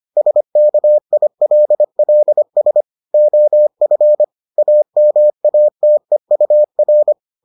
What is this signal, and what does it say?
Morse code, heard as a keyed tone, about 25 wpm, 600 Hz. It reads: SKILLS OF AMATEUR